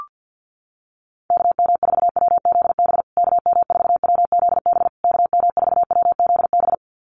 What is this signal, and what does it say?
Morse code, heard as a keyed tone, about 50 wpm, 700 Hz. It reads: KM4WZB KM4WZB KM4WZB
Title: USA Call Signs Repeated 3x 50wpm